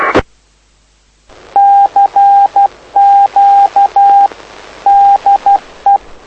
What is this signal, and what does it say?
Morse code, heard as a keyed tone, about 12 wpm, 800 Hz. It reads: CQ DE
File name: lu4aao / lu7aa 09dic2011 exp planeador vuelo3